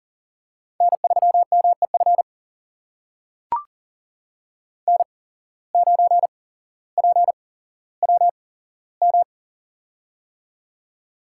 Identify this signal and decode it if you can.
Morse code, heard as a keyed tone, about 40 wpm, spaced as for 17 wpm, 700 Hz. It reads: N3MEF N9PWM
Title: USA Call Signs - 5 Characters - 8x Character Spacing - 40wpm